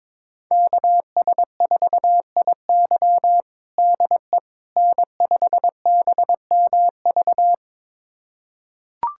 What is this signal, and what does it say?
Morse code, heard as a keyed tone, about 21 wpm, 700 Hz. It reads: KS4IY DE N5BMV
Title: USA Callsign DE Callsign - 5 Characters - 22wpm